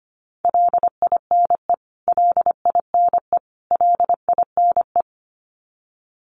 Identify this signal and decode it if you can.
Morse code, heard as a keyed tone, about 25 wpm, 700 Hz. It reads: LINE LINE LINE
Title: Top 300 Words Repeated 25wpm